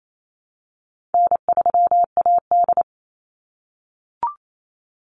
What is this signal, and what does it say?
Morse code, heard as a keyed tone, about 28 wpm, 700 Hz. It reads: N3AD
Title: Easy Call Signs - 28wpm